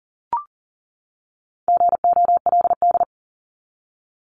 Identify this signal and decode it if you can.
Morse code, heard as a keyed tone, about 40 wpm, 700 Hz. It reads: GOLD
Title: Top 500 Words 40wpm